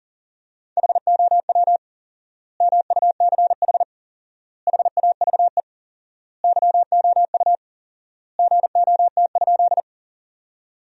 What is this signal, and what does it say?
Morse code, heard as a keyed tone, about 40 wpm, 700 Hz. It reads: HOW MUCH HAVE YOU GOT?
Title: 4x Word-Spacing - Sentences from Top 300 Words - 40wpm